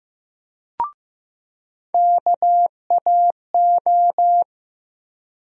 Morse code, heard as a keyed tone, about 15 wpm, 700 Hz. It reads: KAO
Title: USA Call Sign Postfixes - 3 Letters 15wpm